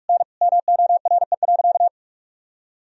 Morse code, heard as a keyed tone, about 46 wpm, 700 Hz. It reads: N MORE.
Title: Sentences from Top 100 Words and Abbreviations 45wpm